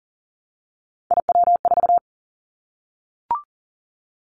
Morse code, heard as a keyed tone, about 37 wpm, 700 Hz. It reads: IW4
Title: Callsign Prefixes for Top 25 DXCC - 40wpm